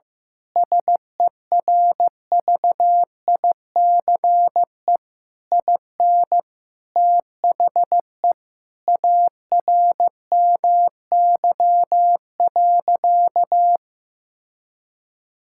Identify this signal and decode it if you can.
Morse code, heard as a keyed tone, about 15 wpm, 700 Hz. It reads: SERVICE IN THE ARMY.
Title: Sentences from Top 1000 Words - Review Part 6 - 15wpm